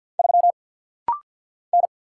Morse code, heard as a keyed tone, about 50 wpm, 700 Hz. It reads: 3 N
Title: Single Letter-Number - Rapid-Fire 50wpm